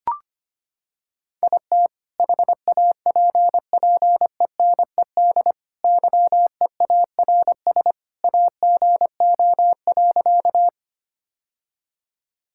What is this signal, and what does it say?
Morse code, heard as a keyed tone, about 25 wpm, 700 Hz. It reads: IT HAPPENED YEARS AGO.
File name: practice / 25wpm / Sentences with 2-Word Phrases - 25wpm